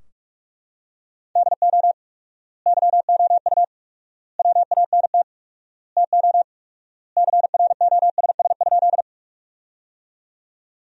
Morse code, heard as a keyed tone, about 42 wpm, 700 Hz. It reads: DO YOU WANT TO CROSS?
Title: 4x Word-Spacing - Sentences from Top 300 Words - 45wpm